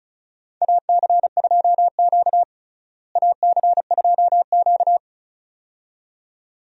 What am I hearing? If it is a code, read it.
Morse code, heard as a keyed tone, about 36 wpm, 700 Hz. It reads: AC2Q AC2Q
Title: USA Call Signs - 4 Characters - Repeated 2x - 35wpm